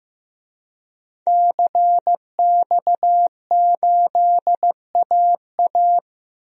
Morse code, heard as a keyed tone, about 15 wpm, 700 Hz. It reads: CX8AA
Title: DXCC-100 Call Signs - 5 Characters - Repeated 3x - 15wpm